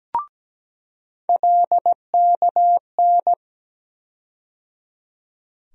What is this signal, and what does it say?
Morse code, heard as a keyed tone, about 17 wpm, 700 Hz. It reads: LKN
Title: USA Call Sign Postfixes - 3 Letters 17wpm